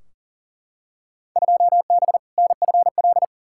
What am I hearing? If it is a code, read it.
Morse code, heard as a keyed tone, about 40 wpm, 700 Hz. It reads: 2B NFL
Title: Field Day Exchanges Encore - 40wpm